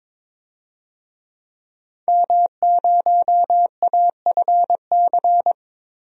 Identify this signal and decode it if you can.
Morse code, heard as a keyed tone, about 22 wpm, 700 Hz. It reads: M0AFC
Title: DXCC-100 Call Signs - 5 Characters - 22wpm